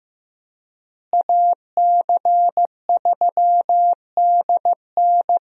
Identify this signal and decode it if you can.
Morse code, heard as a keyed tone, about 15 wpm, 700 Hz. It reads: AC3DN
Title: USA Call Signs - 5 Characters - Repeated 3x - 15wpm